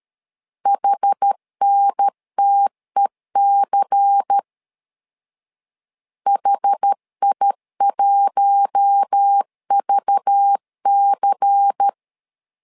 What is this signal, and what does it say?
Morse code, heard as a keyed tone, about 13 wpm, 800 Hz. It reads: HNTEC HI1VC
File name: LEZIONI / 5 LBVC / 5 LBVC 12wpm 1955